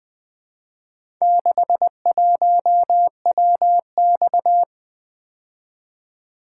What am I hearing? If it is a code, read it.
Morse code, heard as a keyed tone, about 23 wpm, 700 Hz. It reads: 61WX
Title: Speed-Racing - Four Letter-Number Combinations - 20wpm